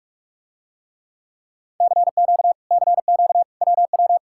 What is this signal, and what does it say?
Morse code, heard as a keyed tone, about 46 wpm, 700 Hz. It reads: CQ CQ WW